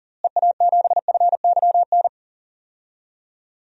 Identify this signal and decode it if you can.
Morse code, heard as a keyed tone, about 40 wpm, 700 Hz. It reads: EA7FYN
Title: DXCC-100 Call Signs - 6 Characters - Repeated 2x - 40wpm